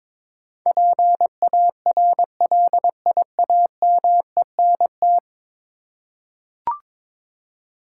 Morse code, heard as a keyed tone, about 21 wpm, 700 Hz. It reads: PARLIAMENT
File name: practice / 22wpm / 10 Letter Words - First 3 Letters Given - 22wpm